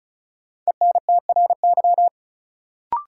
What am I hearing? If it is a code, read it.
Morse code, heard as a keyed tone, about 36 wpm, 700 Hz. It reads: ENTRY
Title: Sentences from Top 1500 Words - New Word Drill Alt - 35wpm